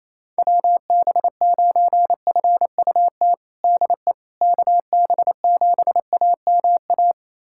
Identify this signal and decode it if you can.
Morse code, heard as a keyed tone, about 28 wpm, 700 Hz. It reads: WB9FUT DE KB7AMA
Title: USA Callsign DE Callsign - 6 Characters - 28wpm